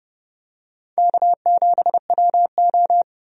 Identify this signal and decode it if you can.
Morse code, heard as a keyed tone, about 30 wpm, 700 Hz. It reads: K7WO
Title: USA Call Signs - 4 Characters - 30wpm